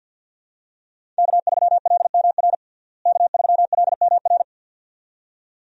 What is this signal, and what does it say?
Morse code, heard as a keyed tone, about 52 wpm, 700 Hz. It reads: K3LMR K3LMR
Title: USA Call Signs - 5 Characters - Repeated 2x - 50wpm